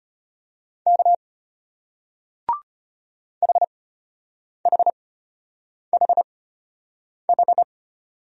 Morse code, heard as a keyed tone, about 34 wpm, spaced as for 8 wpm, 700 Hz. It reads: K HHHH